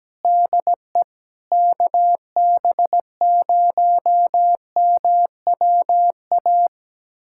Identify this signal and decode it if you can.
Morse code, heard as a keyed tone, about 17 wpm, 700 Hz. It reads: DE KB0MWA
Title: USA Callsign DE Callsign - 6 Characters - 17wpm